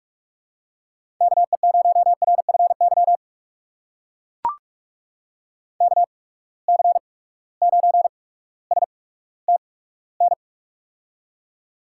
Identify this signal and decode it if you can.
Morse code, heard as a keyed tone, about 44 wpm, spaced as for 19 wpm, 700 Hz. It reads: KE0RFY KC9STN
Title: USA Call Signs - 6 Characters - 8x Character Spacing - 45wpm